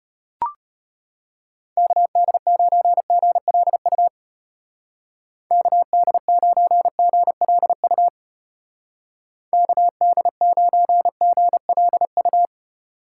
Morse code, heard as a keyed tone, about 30 wpm, 700 Hz. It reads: KD9GLU KD9GLU KD9GLU
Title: Speed Racing - USA 2x3 Call Signs 25wpm